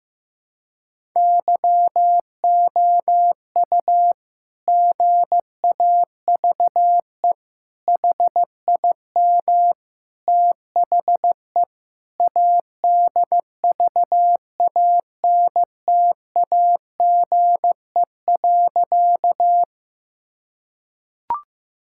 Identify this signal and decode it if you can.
Morse code, heard as a keyed tone, about 15 wpm, 700 Hz. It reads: YOU GAVE HIM THE ADVANTAGE.
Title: Sentences from Top 1000 Words - Review Part 6 - 15wpm